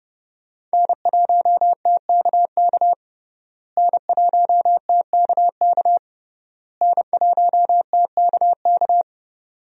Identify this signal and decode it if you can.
Morse code, heard as a keyed tone, about 32 wpm, 700 Hz. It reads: N1TKK N1TKK N1TKK
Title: USA Call Signs - 5 Characters - Repeated 3x - 30wpm